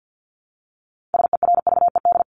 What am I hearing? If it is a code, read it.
Morse code, heard as a keyed tone, about 50 wpm, 700 Hz. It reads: SERVED
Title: ED Suffix - Top 100 Words with Base Words 50wpm